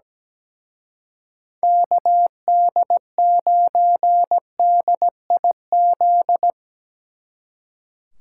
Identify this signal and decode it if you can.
Morse code, heard as a keyed tone, about 18 wpm, 700 Hz. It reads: KD9DIZ
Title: Speed Racing - USA 2x3 Call Signs 17wpm